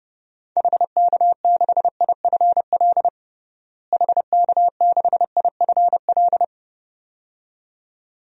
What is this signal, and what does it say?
Morse code, heard as a keyed tone, about 30 wpm, 700 Hz. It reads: HK6IFL HK6IFL
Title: DXCC-100 Call Signs - 6 Characters - Repeated 2x - 30wpm